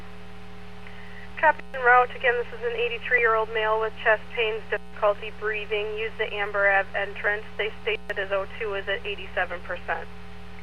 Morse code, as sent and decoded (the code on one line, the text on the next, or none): none